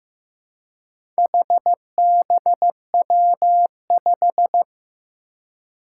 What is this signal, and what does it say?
Morse code, heard as a keyed tone, about 15 wpm, 700 Hz. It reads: HBW5
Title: Four Challenging Letter-Number Combinations 15wpm